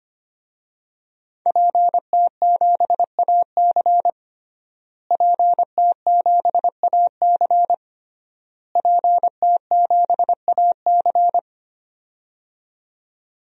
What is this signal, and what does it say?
Morse code, heard as a keyed tone, about 25 wpm, 700 Hz. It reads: PT7AC PT7AC PT7AC
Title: DXCC-100 Call Signs - 5 Characters - Repeated 3x - 25wpm